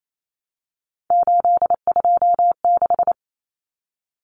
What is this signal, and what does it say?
Morse code, heard as a keyed tone, about 28 wpm, 700 Hz. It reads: 826
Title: Common Cut Numbers - 3-Digit 28wpm